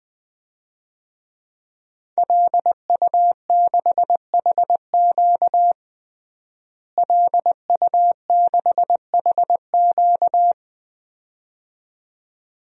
Morse code, heard as a keyed tone, about 20 wpm, 700 Hz. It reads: LU6HQ LU6HQ
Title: DXCC-100 Call Signs - 5 Characters - Repeated 2x - 20wpm